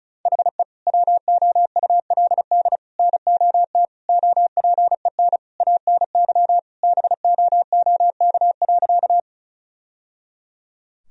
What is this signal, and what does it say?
Morse code, heard as a keyed tone, about 34 wpm, 700 Hz. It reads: HE WOULD NOT OPEN ANY BOOK.
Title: Sentences from Top 1000 Words - Review Part 1 - 35wpm